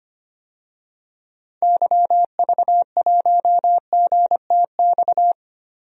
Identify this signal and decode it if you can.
Morse code, heard as a keyed tone, about 25 wpm, 700 Hz. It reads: YV1GTX
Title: DXCC-100 Call Signs - 6 Characters - Repeated 2x - 25wpm